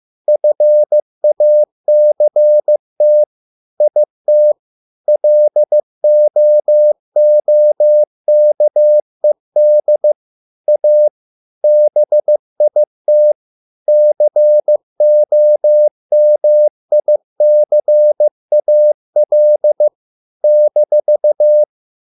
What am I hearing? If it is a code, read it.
Morse code, heard as a keyed tone, about 15 wpm, 600 Hz. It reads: FACT IT LOOKED A BIT COMICAL -